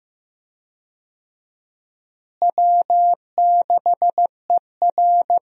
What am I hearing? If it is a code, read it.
Morse code, heard as a keyed tone, about 15 wpm, 700 Hz. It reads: W6ER
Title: USA Call Signs - 4 Characters - Repeated 2x - 15wpm